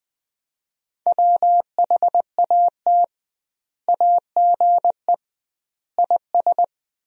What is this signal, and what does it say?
Morse code, heard as a keyed tone, about 20 wpm, 700 Hz. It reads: WHAT AGE IS
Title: 2x Word-Spacing - Sentences from Top 500 Words - 20wpm